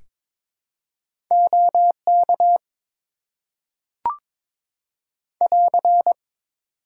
Morse code, heard as a keyed tone, about 22 wpm, 700 Hz. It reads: OK <AR>
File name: Top 90 QSO Elements - 6x Character Spacing - 22wpm